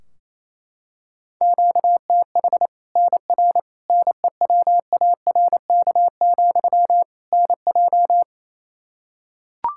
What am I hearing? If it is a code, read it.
Morse code, heard as a keyed tone, about 28 wpm, 700 Hz. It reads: QTH NR NEWARK, NJ
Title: Practice QSO - Exchange 1 - Beginning - 28wpm